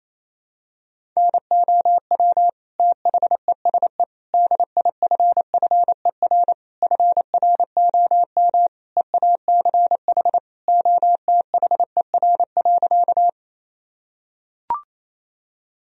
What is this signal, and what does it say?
Morse code, heard as a keyed tone, about 27 wpm, 700 Hz. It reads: NOW THESE DIFFER FROM EACH OTHER.
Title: Sentences from Top 1000 Words - Review Part 1 - 28wpm